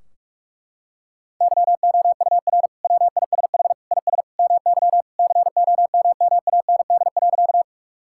Morse code, heard as a keyed tone, about 41 wpm, 700 Hz. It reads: YOUR WISH IS MY COMMAND.
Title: Sentences from Top 1600 Words 45wpm